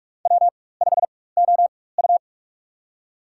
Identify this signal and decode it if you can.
Morse code, heard as a keyed tone, about 46 wpm, spaced as for 21 wpm, 700 Hz. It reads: W5OU
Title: USA Call Signs - 4 Characters - 4x Character Spacing - 45wpm